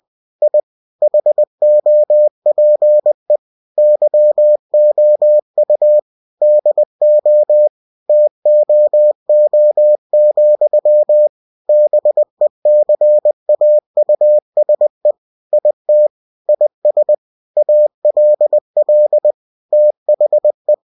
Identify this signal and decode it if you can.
Morse code, heard as a keyed tone, about 20 wpm, 600 Hz. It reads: I HOPE YOU DO TOO, BECAUSE IT IS ALL THE